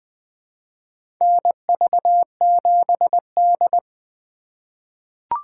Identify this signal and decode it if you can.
Morse code, heard as a keyed tone, about 20 wpm, 700 Hz. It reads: NV7D
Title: USA Call Signs - 4 Characters - 2x Character Spacing - 20wpm